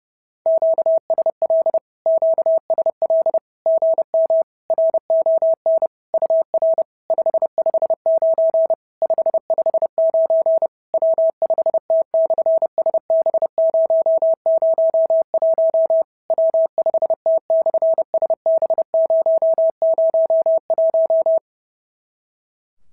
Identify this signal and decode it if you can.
Morse code, heard as a keyed tone, about 30 wpm, 650 Hz. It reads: QSL QSL GM RON UR 559 559 W5T/SB001 W5T/SB001
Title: SOTA QSO Practice 30wpm